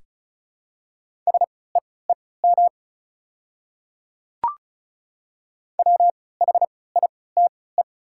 Morse code, heard as a keyed tone, about 36 wpm, spaced as for 20 wpm, 700 Hz. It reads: SEEM WHITE
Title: 3x Character Spacing - Top 300 Words - 35wpm